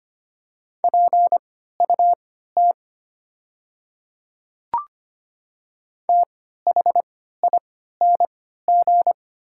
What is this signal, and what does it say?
Morse code, heard as a keyed tone, about 25 wpm, spaced as for 14 wpm, 700 Hz. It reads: PUT THING
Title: 3x Character Spacing - Top 300 Words - 25wpm